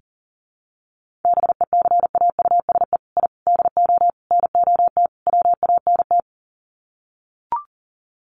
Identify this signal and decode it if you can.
Morse code, heard as a keyed tone, about 40 wpm, 700 Hz. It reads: BECAUSE I DO NOT WANT
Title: Sets of 5 Words Encore 40wpm